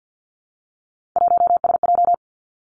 Morse code, heard as a keyed tone, about 50 wpm, 700 Hz. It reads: 1SJ
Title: Three Challenging Letter-Number Combinations 50wpm